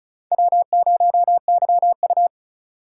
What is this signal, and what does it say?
Morse code, heard as a keyed tone, about 36 wpm, 700 Hz. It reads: W0YU
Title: USA Call Signs - 4x Character Spacing - 35wpm